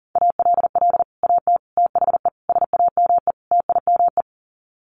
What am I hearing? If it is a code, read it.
Morse code, heard as a keyed tone, about 40 wpm, 700 Hz. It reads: ALL AT THE SAME TIME
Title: Sets of 5 Words Encore2 40wpm